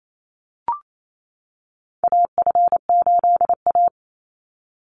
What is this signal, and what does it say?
Morse code, heard as a keyed tone, about 28 wpm, 700 Hz. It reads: AF8A
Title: Call Signs - 28wpm